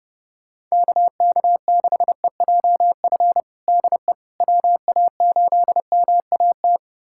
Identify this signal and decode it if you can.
Morse code, heard as a keyed tone, about 30 wpm, 700 Hz. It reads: KK6EJF DE WA8MAT
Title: USA Callsign DE Callsign 30wpm